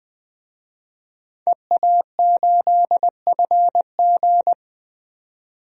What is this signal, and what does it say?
Morse code, heard as a keyed tone, about 20 wpm, 700 Hz. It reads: EA8FG
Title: DXCC-100 Call Signs - 5 Characters - 20wpm